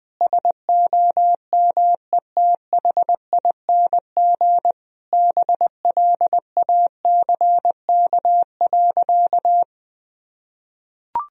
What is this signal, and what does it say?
Morse code, heard as a keyed tone, about 21 wpm, 700 Hz. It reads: SOMETHING BLACK.
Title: Sentences from Top 1000 Words - Review Part 2 - 20wpm